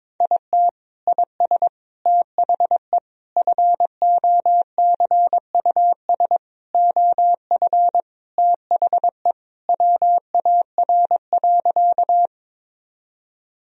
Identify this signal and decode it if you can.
Morse code, heard as a keyed tone, about 22 wpm, 700 Hz. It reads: IT IS THE FOCUS OF THE WAR.
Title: Sentences from Top 1000 Words - Review Part 4 - 22wpm